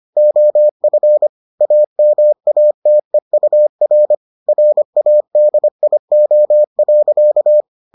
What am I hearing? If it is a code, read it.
Morse code, heard as a keyed tone, about 25 wpm, 600 Hz. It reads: OF AMATEUR RADIO.